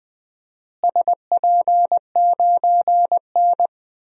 Morse code, heard as a keyed tone, about 20 wpm, 700 Hz. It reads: SP9N
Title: DXCC-100 Call Signs - 4 Characters - Repeated 2x - 20wpm